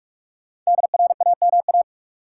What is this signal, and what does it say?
Morse code, heard as a keyed tone, about 46 wpm, 700 Hz. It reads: DRAMA
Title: Sentences from Top 1500 Words - New Word Drill Alt - 45wpm